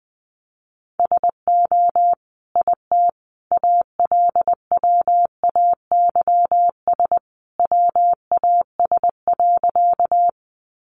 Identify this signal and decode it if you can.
Morse code, heard as a keyed tone, about 20 wpm, 700 Hz. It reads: SO IT ALWAYS WAS.
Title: Sentences from Top 300 Words 20wpm